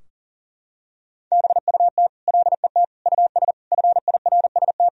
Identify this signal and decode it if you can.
Morse code, heard as a keyed tone, about 40 wpm, 700 Hz. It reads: BUT LET US FIRST
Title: Sentences from Top 1000 Words - Review Part 4 - 40wpm